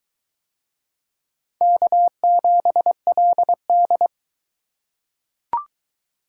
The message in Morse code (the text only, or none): K7LD